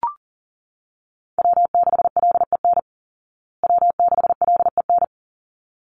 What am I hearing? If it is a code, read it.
Morse code, heard as a keyed tone, about 40 wpm, 700 Hz. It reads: W6LEN W6LEN
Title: Call Signs Repeated 3x - 40wpm